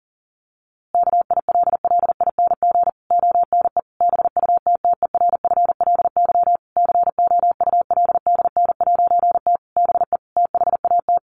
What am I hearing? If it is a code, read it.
Morse code, heard as a keyed tone, about 40 wpm, 700 Hz. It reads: KILLING ONE BUTTERFLY COULDN'T BE THAT
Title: A Sound Of Thunder 40wpm